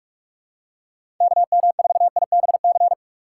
Morse code, heard as a keyed tone, about 44 wpm, 700 Hz. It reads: KM4IBC
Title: USA Call Signs - 6 Characters - 6x Character Spacing - 45wpm